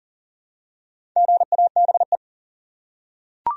Morse code, heard as a keyed tone, about 40 wpm, 700 Hz. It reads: GABE
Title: USA Common Names - Max 4 Letters - 40wpm